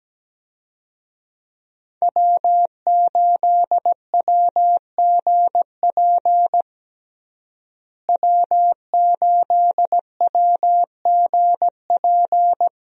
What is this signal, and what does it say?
Morse code, heard as a keyed tone, about 17 wpm, 700 Hz. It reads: W8WGP W8WGP
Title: USA Call Signs - 5 Characters - Repeated 3x - 17wpm